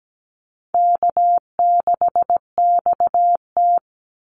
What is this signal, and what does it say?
Morse code, heard as a keyed tone, about 17 wpm, 700 Hz. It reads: K6XT
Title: Easy Call Signs Repeated 2x - 17wpm